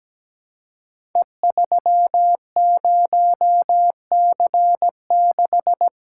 Morse code, heard as a keyed tone, about 17 wpm, 700 Hz. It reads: E30C6
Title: Code Groups - 5 Letter:Numbers - 17wpm